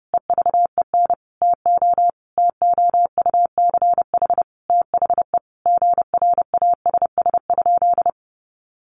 Morse code, heard as a keyed tone, about 30 wpm, 700 Hz. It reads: EVEN TO TOUCH THE GRASS?
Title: A Sound Of Thunder 30wpm